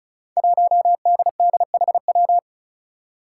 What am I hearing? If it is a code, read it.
Morse code, heard as a keyed tone, about 36 wpm, 700 Hz. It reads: 1DDHW
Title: Five Challenging Letter-Number Combinations 35wpm